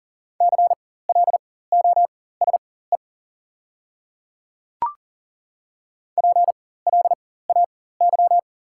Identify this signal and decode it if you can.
Morse code, heard as a keyed tone, about 40 wpm, spaced as for 19 wpm, 700 Hz. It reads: CLOSE PLAY
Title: 4x Character Spacing - Top 300 Words - 40wpm